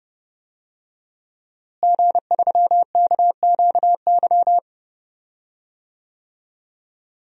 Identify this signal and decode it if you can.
Morse code, heard as a keyed tone, about 30 wpm, 700 Hz. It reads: G3KQY
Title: DXCC-100 Call Signs - 5 Characters - Repeated 3x - 30wpm